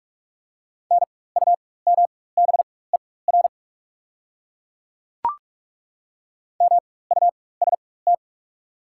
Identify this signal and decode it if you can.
Morse code, heard as a keyed tone, about 43 wpm, spaced as for 21 wpm, 690 Hz. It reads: NUMBER MUST
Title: 4x Character Spacing - Top 300 Words - 45wpm